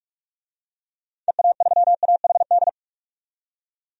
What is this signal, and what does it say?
Morse code, heard as a keyed tone, about 42 wpm, 700 Hz. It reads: EA3AHD
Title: DXCC-100 Call Signs - 6 Characters - 45wpm